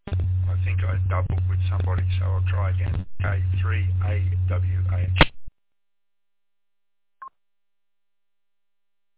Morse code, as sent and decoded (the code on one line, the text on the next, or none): none